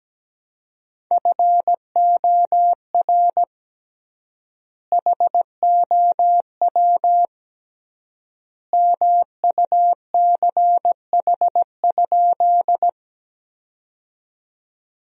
Morse code, heard as a keyed tone, about 17 wpm, 700 Hz. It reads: FOR HOW MUCH?
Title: 3x Word-Spacing - Sentences from Top 200 Words - 17wpm